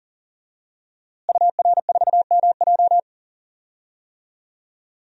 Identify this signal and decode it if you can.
Morse code, heard as a keyed tone, about 40 wpm, 700 Hz. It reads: UR4MJ
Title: DXCC-100 Call Signs - 5 Characters - Repeated 2x - 40wpm